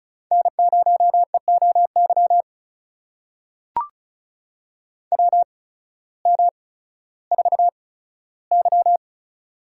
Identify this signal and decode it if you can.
Morse code, heard as a keyed tone, about 36 wpm, spaced as for 16 wpm, 700 Hz. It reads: N0EOY WM4Y
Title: USA Call Signs - 8x Character Spacing - 35wpm